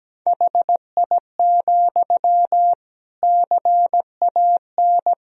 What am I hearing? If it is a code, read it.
Morse code, heard as a keyed tone, about 17 wpm, 700 Hz. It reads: HI, CAN
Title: Sentences from Top 1000 Words - Review Part 3 - 17wpm